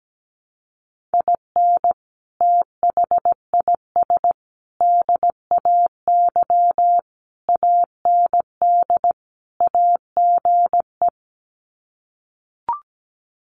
Morse code, heard as a keyed tone, about 17 wpm, 700 Hz. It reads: IN THIS DAY AND AGE
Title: Sets of 5 Words Encore 17wpm